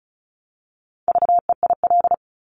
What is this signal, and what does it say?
Morse code, heard as a keyed tone, about 36 wpm, 700 Hz. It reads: VEIL
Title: Multi-letter Phonemes Repeated 3x 35wpm